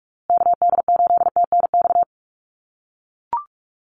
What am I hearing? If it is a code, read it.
Morse code, heard as a keyed tone, about 44 wpm, 690 Hz. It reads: KD8TNX
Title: USA Call Signs Repeated 3x 45wpm